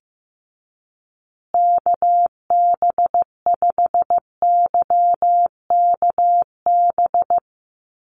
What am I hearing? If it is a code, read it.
Morse code, heard as a keyed tone, about 15 wpm, 700 Hz. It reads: KB5YKB
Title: USA Call Signs Repeated 2x 15wpm